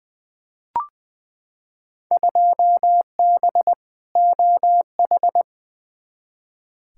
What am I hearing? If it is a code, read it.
Morse code, heard as a keyed tone, about 20 wpm, 700 Hz. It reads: 2B OH
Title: Field Day Exchanges Encore - 20wpm